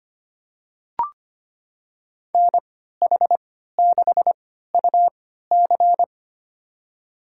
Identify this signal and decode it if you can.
Morse code, heard as a keyed tone, about 25 wpm, spaced as for 14 wpm, 700 Hz. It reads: NH6UC
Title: USA Call Signs - 5 Characters - 3x Character Spacing - 25wpm